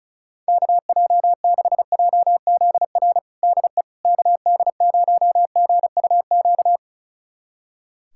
Morse code, heard as a keyed tone, about 36 wpm, 700 Hz. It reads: KJ6JZR DE KD0GUQ
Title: USA Callsign DE Callsign 35wpm